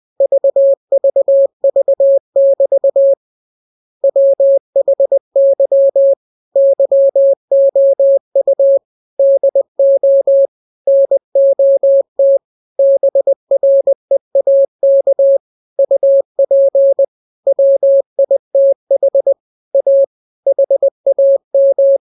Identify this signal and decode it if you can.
Morse code, heard as a keyed tone, about 20 wpm, 550 Hz. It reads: VVV<BT> WHY YOU DO NOT BREAK UP WITH A HAM